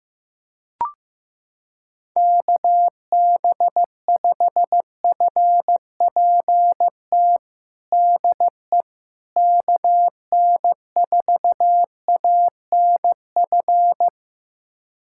Morse code, heard as a keyed tone, about 15 wpm, 700 Hz. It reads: KB5FPT DE KN4ANF
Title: USA Callsign DE Callsign - 6 Characters - 15wpm